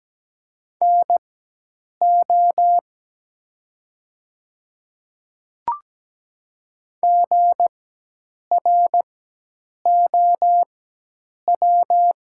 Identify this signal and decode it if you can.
Morse code, heard as a keyed tone, about 17 wpm, spaced as for 8 wpm, 700 Hz. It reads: NO GROW
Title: 4x Character Spacing - Top 500 Words - 17wpm